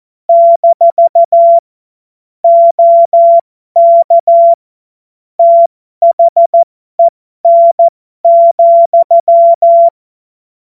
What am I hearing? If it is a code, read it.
Morse code, heard as a keyed tone, about 14 wpm, spaced as for 12 wpm, 680 Hz. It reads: - OK THEN,
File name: Morsekurs / Witze